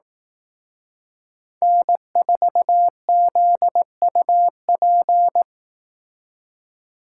Morse code, heard as a keyed tone, about 18 wpm, 700 Hz. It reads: N4ZUP
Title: Speed Racing - USA 1x3 Call Signs 15wpm